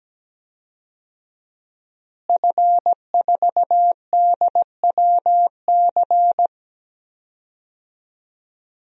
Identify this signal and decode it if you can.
Morse code, heard as a keyed tone, about 17 wpm, 700 Hz. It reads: F4DWC
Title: DXCC-100 Call Signs - 5 Characters - Repeated 2x - 17wpm